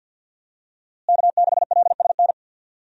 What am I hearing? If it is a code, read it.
Morse code, heard as a keyed tone, about 52 wpm, 700 Hz. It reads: K6LSN
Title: USA Call Signs - 5 Characters - Repeated 2x - 50wpm